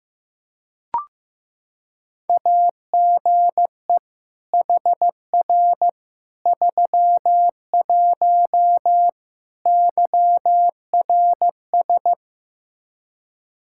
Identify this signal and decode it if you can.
Morse code, heard as a keyed tone, about 15 wpm, 700 Hz. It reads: AGE HR 31 YRS
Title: Practice QSO - Exchange 3 - More Info - 15wpm